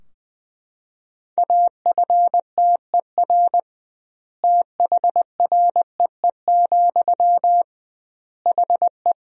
Morse code, heard as a keyed tone, about 21 wpm, 700 Hz. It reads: AFTER THREE, HE
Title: 2x Word-Spacing - Sentences from Top 200 Words - 20wpm